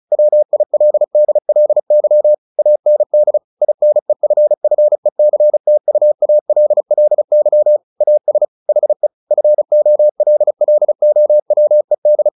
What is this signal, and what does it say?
Morse code, heard as a keyed tone, about 35 wpm, 600 Hz. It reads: WILDLY AND INEFFECTUALLY AS HE FOLLOWED
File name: the old man and the sea / the old man and the sea 1 35wpm